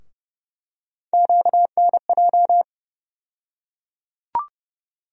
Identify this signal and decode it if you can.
Morse code, heard as a keyed tone, about 30 wpm, 700 Hz. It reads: QNJ
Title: USA Call Sign Postfixes - 3 Letters 30wpm